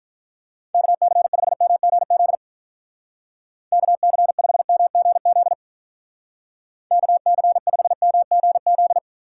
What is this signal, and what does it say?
Morse code, heard as a keyed tone, about 44 wpm, 700 Hz. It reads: KC5MGZ KC5MGZ KC5MGZ